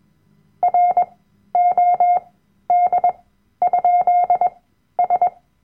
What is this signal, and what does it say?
Morse code, heard as a keyed tone, about 21 wpm, spaced as for 12 wpm, 710 Hz. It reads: ROD?S
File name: Workshop Dateien / CW-Online-Kurs / Lektion 010